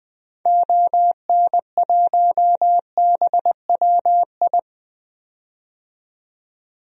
Morse code, heard as a keyed tone, about 20 wpm, 700 Hz. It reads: ON1BWI
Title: DXCC-100 Call Signs - 6 Characters - 20wpm